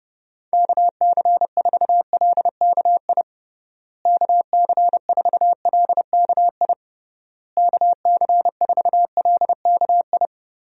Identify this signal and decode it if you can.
Morse code, heard as a keyed tone, about 30 wpm, 700 Hz. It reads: KC4LKI KC4LKI KC4LKI
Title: USA Call Signs - 6 Characters - Repeated 3x - 30wpm